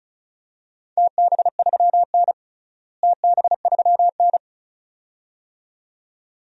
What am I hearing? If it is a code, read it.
Morse code, heard as a keyed tone, about 36 wpm, 700 Hz. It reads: TB3N TB3N
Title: DXCC-100 Call Signs - 4 Characters - Repeated 2x - 35wpm